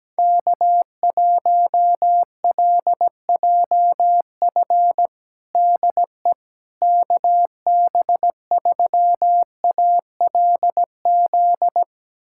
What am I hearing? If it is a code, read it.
Morse code, heard as a keyed tone, about 17 wpm, 700 Hz. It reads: K1LJF DE KB3ALZ
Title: USA Callsign DE Callsign 17wpm